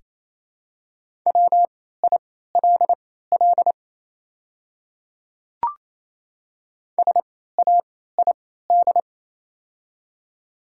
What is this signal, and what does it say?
Morse code, heard as a keyed tone, about 28 wpm, spaced as for 16 wpm, 700 Hz. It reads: WILL SAID
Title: 3x Character Spacing - Top 200 Words - 28wpm